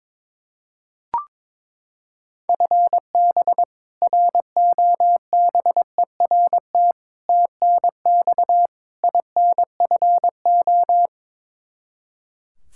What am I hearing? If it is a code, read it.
Morse code, heard as a keyed tone, about 22 wpm, 700 Hz. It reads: FB ROBERT TNX INFO
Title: K1USN SST QSO Practice 22wpm